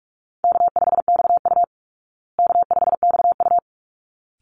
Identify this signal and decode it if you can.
Morse code, heard as a keyed tone, about 44 wpm, 700 Hz. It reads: K5XU K5XU
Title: Easy Call Signs Repeated 2x - 45wpm